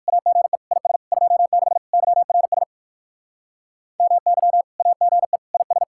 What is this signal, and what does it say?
Morse code, heard as a keyed tone, about 59 wpm, 700 Hz. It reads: AGE IS 26 YRS MY AGE IS